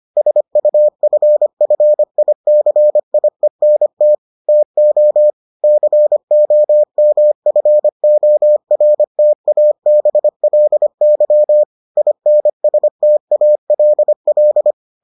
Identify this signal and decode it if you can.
Morse code, heard as a keyed tone, about 25 wpm, 600 Hz. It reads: SUFFICIENT TO COMFORTABLY INSTALL